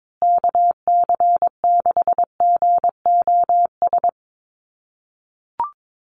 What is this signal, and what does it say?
Morse code, heard as a keyed tone, about 22 wpm, 700 Hz. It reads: KC6GOS
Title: USA Call Signs Repeated 3x 22wpm